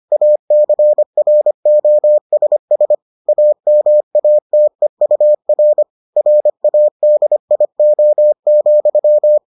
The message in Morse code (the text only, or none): ACROSS AMATEUR RADIO,